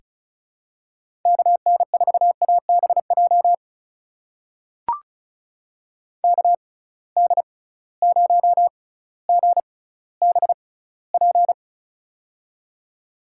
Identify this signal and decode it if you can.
Morse code, heard as a keyed tone, about 36 wpm, spaced as for 18 wpm, 700 Hz. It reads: KN4ABJ KD0GBP
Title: USA Call Signs - 6 Characters - 6x Character Spacing - 35wpm